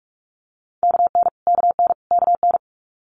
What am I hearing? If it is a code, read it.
Morse code, heard as a keyed tone, about 46 wpm, 700 Hz. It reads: KN KN KN
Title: Multi-letter Phonemes Repeated 3x 45wpm